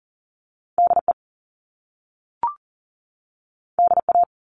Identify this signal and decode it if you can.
Morse code, heard as a keyed tone, about 40 wpm, 700 Hz. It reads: DE DA
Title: Top Two Letter Combinations in Words 40wpm